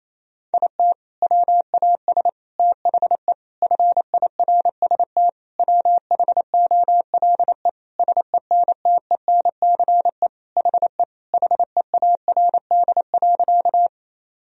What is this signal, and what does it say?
Morse code, heard as a keyed tone, about 28 wpm, 700 Hz. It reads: IT WAS THE FIRST WHOLE SENTENCE HE HEARD.
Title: Sentences from Top 1000 Words - Review Part 1 - 28wpm